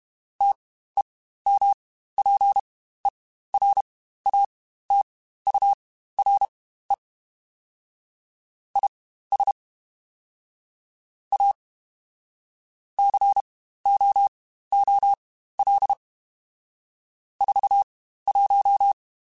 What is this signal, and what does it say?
Morse code, heard as a keyed tone, about 32 wpm, spaced as for 15 wpm, 800 Hz. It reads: TEMPERATURE IS A COOL 41
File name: Spaced Characters / Spaced Characters-mp3s / Weather 4 14 sp @30